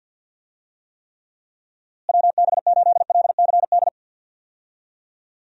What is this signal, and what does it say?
Morse code, heard as a keyed tone, about 46 wpm, 700 Hz. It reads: WB8LCD
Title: USA Call Signs - 6 Characters - Repeated 2x - 50wpm